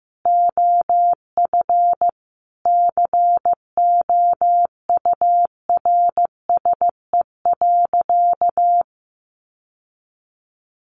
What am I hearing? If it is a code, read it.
Morse code, heard as a keyed tone, about 15 wpm, 700 Hz. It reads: OF COURSE.
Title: Sentences from Top 800 Words 15wpm